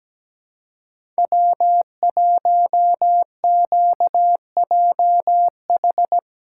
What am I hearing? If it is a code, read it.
Morse code, heard as a keyed tone, about 17 wpm, 700 Hz. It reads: W1QJH
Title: USA Callsign DE Callsign 17wpm